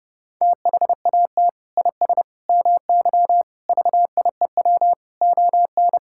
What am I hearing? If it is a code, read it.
Morse code, heard as a keyed tone, about 30 wpm, 700 Hz. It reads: THAT IS MY VIEW ON
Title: Sentences from Top 1000 Words - Review Part 6 - 30wpm